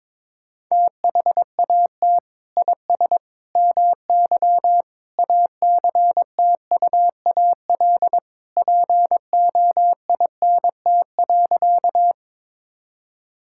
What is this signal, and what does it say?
Morse code, heard as a keyed tone, about 22 wpm, 700 Hz. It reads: THAT IS MY ACTUAL POINT.
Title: Sentences from Top 1000 Words - Review Part 5 - 22wpm